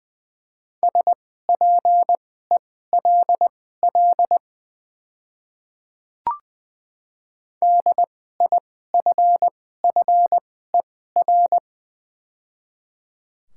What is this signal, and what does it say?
Morse code, heard as a keyed tone, about 20 wpm, spaced as for 14 wpm, 700 Hz. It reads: SPELL DIFFER
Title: 2x Character Spacing - Top 300 Words - 20wpm